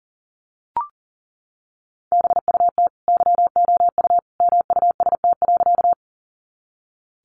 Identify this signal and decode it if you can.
Morse code, heard as a keyed tone, about 40 wpm, 700 Hz. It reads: BUT YOU MUST.
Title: Sentences from Top 200 Words 40wpm